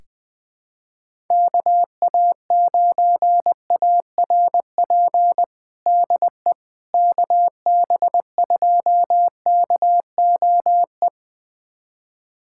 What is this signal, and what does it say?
Morse code, heard as a keyed tone, about 20 wpm, 700 Hz. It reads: KA9ARP DE KB2KOE
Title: USA Callsign DE Callsign - 6 Characters - 20wpm